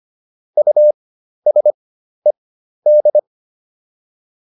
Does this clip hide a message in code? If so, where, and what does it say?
Yes: Morse code from 0.6 to 3.2 s, heard as a keyed tone, about 25 wpm, spaced as for 12 wpm, 600 Hz. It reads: USED